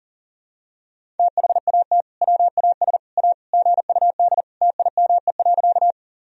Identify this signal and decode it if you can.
Morse code, heard as a keyed tone, about 40 wpm, 700 Hz. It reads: THAT WAS A GUD TIME.
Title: Sentences from Top 200 Words and Abbreviations 40wpm